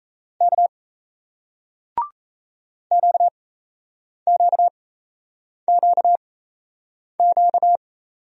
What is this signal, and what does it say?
Morse code, heard as a keyed tone, about 38 wpm, spaced as for 8 wpm, 700 Hz. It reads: K QQQQ